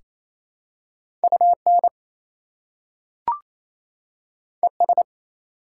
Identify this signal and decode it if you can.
Morse code, heard as a keyed tone, about 28 wpm, 700 Hz. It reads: UN ES
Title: Prefixes and Suffixes 28wpm